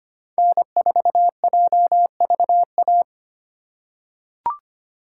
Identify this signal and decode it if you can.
Morse code, heard as a keyed tone, about 25 wpm, 700 Hz. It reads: N4JVA
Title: USA Call Signs - 3x Character Spacing - 25wpm